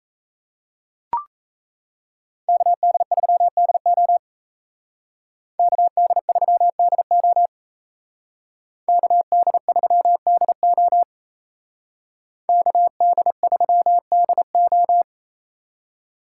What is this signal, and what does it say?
Morse code, heard as a keyed tone, about 36 wpm, 700 Hz. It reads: KD3DO KD3DO KD3DO KD3DO